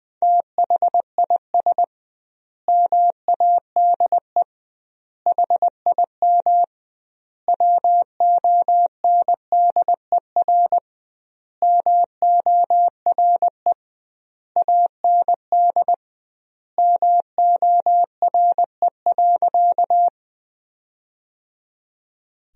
Morse code, heard as a keyed tone, about 21 wpm, 700 Hz. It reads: THIS MADE HIM WONDER MORE AND MORE.
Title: 2x Word-Spacing - Sentences from Top 500 Words - 20wpm